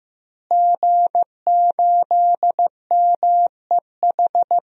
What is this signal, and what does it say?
Morse code, heard as a keyed tone, about 15 wpm, 700 Hz. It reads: G8MEH
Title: DXCC-100 Call Signs - 5 Characters - Repeated 3x - 15wpm